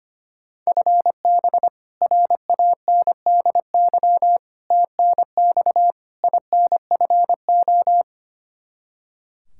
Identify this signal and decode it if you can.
Morse code, heard as a keyed tone, about 25 wpm, 700 Hz. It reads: FB RANDY TNX INFO
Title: K1USN SST QSO Practice 25wpm